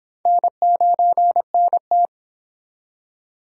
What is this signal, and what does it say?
Morse code, heard as a keyed tone, about 25 wpm, 700 Hz. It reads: N9NT